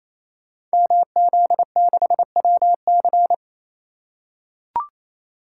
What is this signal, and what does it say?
Morse code, heard as a keyed tone, about 29 wpm, 700 Hz. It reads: MZ6WC
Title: Code Groups - 5 Letter:Numbers - 28wpm